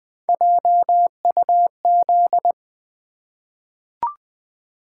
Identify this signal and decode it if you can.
Morse code, heard as a keyed tone, about 20 wpm, 700 Hz. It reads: JUZ
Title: Three Letter-Number Combinations - 20wpm